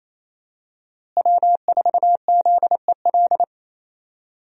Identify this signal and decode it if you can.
Morse code, heard as a keyed tone, about 28 wpm, 700 Hz. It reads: W4ZEL
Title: USA Call Signs - 5 Characters - Repeated 2x - 28wpm